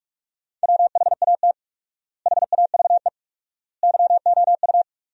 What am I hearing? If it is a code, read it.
Morse code, heard as a keyed tone, about 46 wpm, 700 Hz. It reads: WHAT HAVE YOU
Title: 4x Word-Spacing - Sentences from Top 400 Words - 45wpm